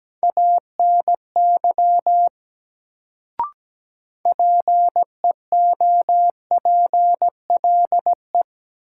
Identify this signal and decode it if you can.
Morse code, heard as a keyed tone, about 17 wpm, 700 Hz. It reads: ANY PEOPLE
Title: Sentences from Top 200 Words - New Word Drill - 17wpm